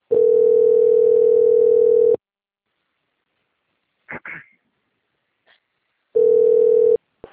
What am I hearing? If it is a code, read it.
Ringback tone (North America), 440 + 480 Hz.
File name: out- - - - - .13390